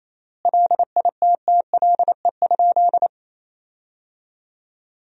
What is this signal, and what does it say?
Morse code, heard as a keyed tone, about 27 wpm, 700 Hz. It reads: LITTLE?
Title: 3x Word-Spacing - Sentences from Top 500 Words - 28wpm